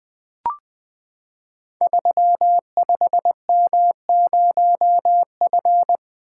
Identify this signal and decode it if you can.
Morse code, heard as a keyed tone, about 20 wpm, 700 Hz. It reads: 35M0F
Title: Code Groups - 5 Letter:Numbers - 20wpm